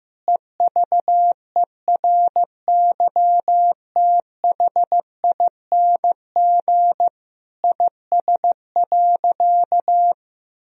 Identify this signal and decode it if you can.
Morse code, heard as a keyed tone, about 15 wpm, 700 Hz. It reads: EVERYTHING IS.
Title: Sentences from Top 1000 Words - Review Part 4 - 15wpm